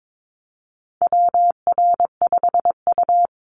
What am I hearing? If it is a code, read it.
Morse code, heard as a keyed tone, about 22 wpm, 700 Hz. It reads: WR5U
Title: Call Signs - 22wpm